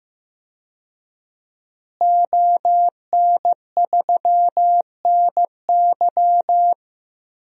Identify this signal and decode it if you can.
Morse code, heard as a keyed tone, about 15 wpm, 700 Hz. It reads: ON3NY
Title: DXCC-100 Call Signs - 5 Characters - Repeated 3x - 15wpm